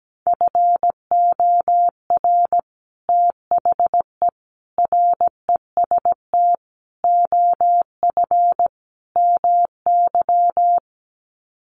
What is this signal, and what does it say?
Morse code, heard as a keyed tone, about 17 wpm, 700 Hz. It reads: FOR THE REST OF MY
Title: Sets of 5 Words 17wpm